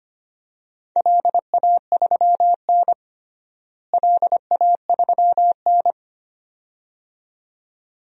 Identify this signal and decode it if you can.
Morse code, heard as a keyed tone, about 25 wpm, 700 Hz. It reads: LA3N LA3N
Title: DXCC-100 Call Signs - 4 Characters - Repeated 2x - 25wpm